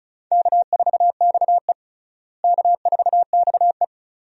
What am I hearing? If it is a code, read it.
Morse code, heard as a keyed tone, about 36 wpm, 700 Hz. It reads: K4XE K4XE
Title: USA Call Signs - 4 Characters - Repeated 2x - 35wpm